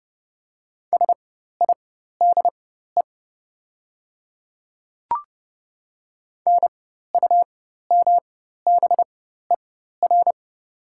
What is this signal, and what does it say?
Morse code, heard as a keyed tone, about 30 wpm, spaced as for 14 wpm, 700 Hz. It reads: SIDE NUMBER
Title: 4x Character Spacing - Top 300 Words - 30wpm